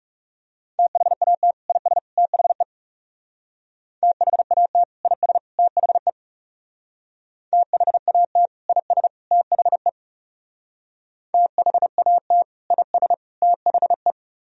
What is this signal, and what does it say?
Morse code, heard as a keyed tone, about 40 wpm, 700 Hz. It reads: THAT IS THE THAT IS THE THAT IS THE THAT IS THE